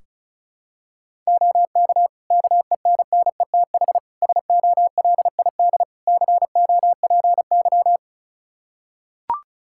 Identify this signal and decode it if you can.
Morse code, heard as a keyed tone, about 34 wpm, 700 Hz. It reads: OK KENNETH SOLID COPY
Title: Practice QSO - Exchange 3 - More Info - 35wpm